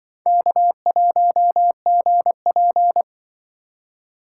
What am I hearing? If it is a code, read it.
Morse code, heard as a keyed tone, about 26 wpm, 700 Hz. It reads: K1GP